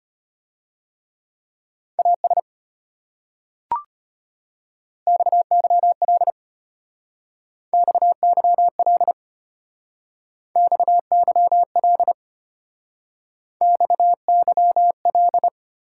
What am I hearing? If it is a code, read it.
Morse code, heard as a keyed tone, about 31 wpm, 700 Hz. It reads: AS XYL XYL XYL XYL
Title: Speed-Racing Top 180 QSO Elements 25wpm